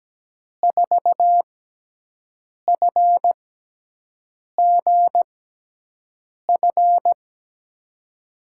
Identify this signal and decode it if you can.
Morse code, heard as a keyed tone, about 17 wpm, spaced as for 6 wpm, 700 Hz. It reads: 4FGF
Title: USA Call Signs - 6 Characters - 6x Character Spacing - 17wpm